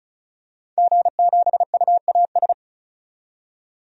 Morse code, heard as a keyed tone, about 36 wpm, 700 Hz. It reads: G7UAS
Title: DXCC-100 Call Signs - 5 Characters - 35wpm